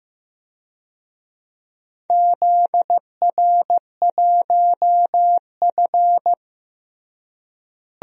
Morse code, heard as a keyed tone, about 15 wpm, 700 Hz. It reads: ZR1F
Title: DXCC-100 Call Signs - 4 Characters - Repeated 2x - 15wpm